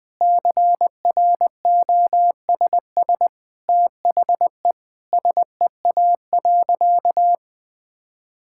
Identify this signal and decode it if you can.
Morse code, heard as a keyed tone, about 20 wpm, 700 Hz. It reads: CROSS THE SEA.
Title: Sentences from Top 1000 Words - Review Part 6 - 20wpm